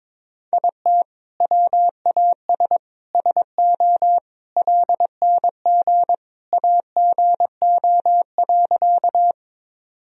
Morse code, heard as a keyed tone, about 22 wpm, 700 Hz. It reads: IT WAS SO LNG AGO.
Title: Sentences from Top 500 Words and Abbreviations 22wpm